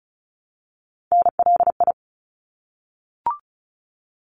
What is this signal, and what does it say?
Morse code, heard as a keyed tone, about 34 wpm, 700 Hz. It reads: NLI
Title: ARRL Sections 35wpm